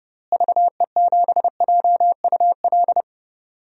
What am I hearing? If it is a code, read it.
Morse code, heard as a keyed tone, about 32 wpm, 700 Hz. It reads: VE7JUL
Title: DXCC-100 Call Signs - 6 Characters - Repeated 3x - 30wpm